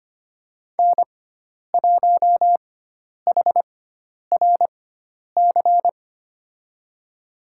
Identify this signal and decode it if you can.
Morse code, heard as a keyed tone, about 25 wpm, spaced as for 10 wpm, 700 Hz. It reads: N1HRC
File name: Lesson 16g - Letter c - Call Signs 25wpm